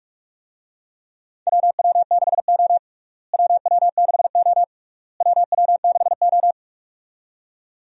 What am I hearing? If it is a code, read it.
Morse code, heard as a keyed tone, about 42 wpm, 700 Hz. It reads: WW6O WW6O WW6O
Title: USA Call Signs - 4 Characters - Repeated 3x - 45wpm